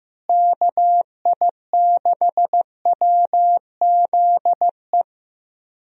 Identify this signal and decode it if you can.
Morse code, heard as a keyed tone, about 15 wpm, 700 Hz. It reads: KI6WZE
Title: USA Call Signs - 6 Characters - Repeated 2x - 15wpm